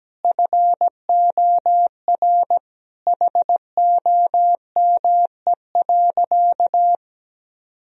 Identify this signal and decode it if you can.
Morse code, heard as a keyed tone, about 17 wpm, 700 Hz. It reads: FOR HOME.
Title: Sentences from Top 1000 Words - Review Part 1 - 17wpm